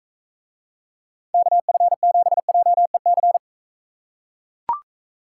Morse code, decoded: KF7JEC

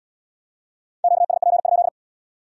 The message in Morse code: QSP?

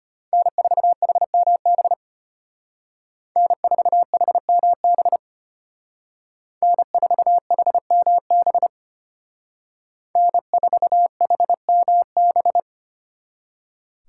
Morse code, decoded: N4HMB N4HMB N4HMB N4HMB

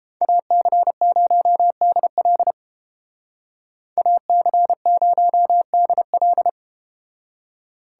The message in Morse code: AC0DL AC0DL

